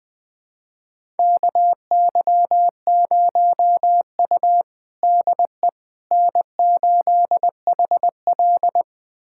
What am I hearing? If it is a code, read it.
Morse code, heard as a keyed tone, about 20 wpm, 700 Hz. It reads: KY0U DE N8HL
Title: USA Callsign DE Callsign - 4 Characters - 20wpm